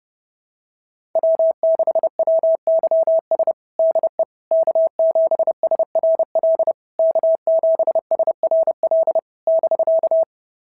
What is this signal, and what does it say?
Morse code, heard as a keyed tone, about 28 wpm, 660 Hz. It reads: W6WYS DE K7SRL K7SRL <BK>
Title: SOTA QSO Practice 30wpm